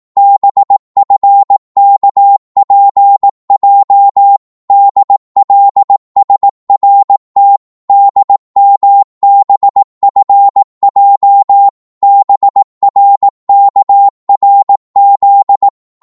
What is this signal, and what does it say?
Morse code, heard as a keyed tone, about 18 wpm, 800 Hz. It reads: BFKPJ DLSRT DMBFJ BRKRZ
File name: Morse / ltr-18wpm